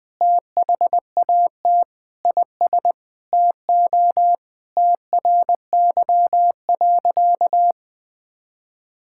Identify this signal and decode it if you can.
Morse code, heard as a keyed tone, about 19 wpm, 700 Hz. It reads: THAT IS TO TRY.
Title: Sentences with 8-Word Phrases - 20wpm